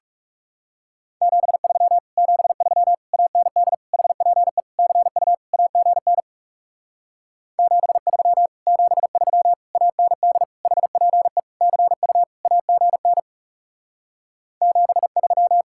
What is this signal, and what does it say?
Morse code, heard as a keyed tone, about 40 wpm, 700 Hz. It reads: 73 73 AND HPE CU AGN 73 73 AND HPE CU AGN 73